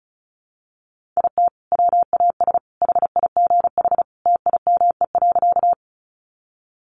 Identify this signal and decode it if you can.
Morse code, heard as a keyed tone, about 34 wpm, 700 Hz. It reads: IT WAS HIGH TIME.
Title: Sentences from Top 200 Words 35wpm